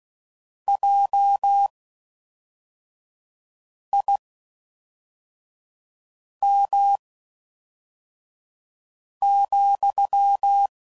Morse code, heard as a keyed tone, about 16 wpm, spaced as for 4 wpm, 800 Hz. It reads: JIM,